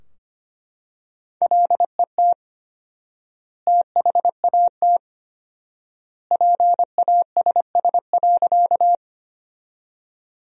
Morse code, heard as a keyed tone, about 25 wpm, 700 Hz. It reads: LET THAT PASS.
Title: 4x Word-Spacing - Sentences from Top 400 Words - 25wpm